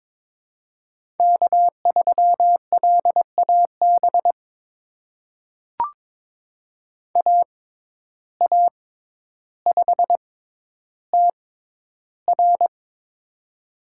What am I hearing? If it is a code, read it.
Morse code, heard as a keyed tone, about 22 wpm, spaced as for 11 wpm, 700 Hz. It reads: K3LAB AA5TR
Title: USA Call Signs - 6x Character Spacing - 22wpm